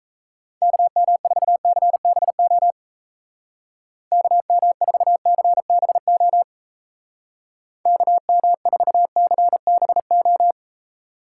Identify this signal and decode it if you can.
Morse code, heard as a keyed tone, about 36 wpm, 700 Hz. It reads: KM4CBO KM4CBO KM4CBO